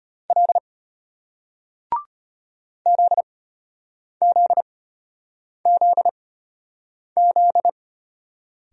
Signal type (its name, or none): Morse code